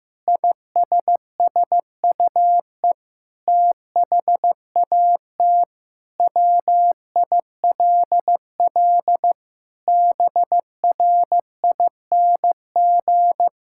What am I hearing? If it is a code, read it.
Morse code, heard as a keyed tone, about 15 wpm, 700 Hz. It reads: ISSUE THAT WILL BRING
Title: Sentences from Top 1000 Words - Review Part 6 - 15wpm